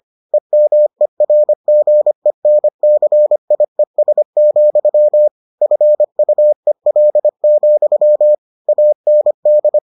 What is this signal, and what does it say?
Morse code, heard as a keyed tone, about 25 wpm, 600 Hz. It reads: EMERGENCIES, FUEL, AND